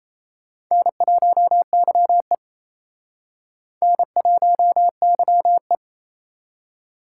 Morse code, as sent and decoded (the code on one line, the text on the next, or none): -. .---- -.-- . / -. .---- -.-- .
N1YE N1YE